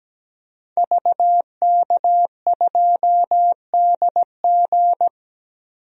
Morse code, heard as a keyed tone, about 17 wpm, 700 Hz. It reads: VK2DG
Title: DXCC-100 Call Signs - 5 Characters - 17wpm